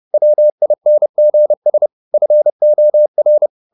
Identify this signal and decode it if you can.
Morse code, heard as a keyed tone, about 30 wpm, 600 Hz. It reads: WINGS FOR
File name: the old man and the sea / the old man and the sea 1 30wpm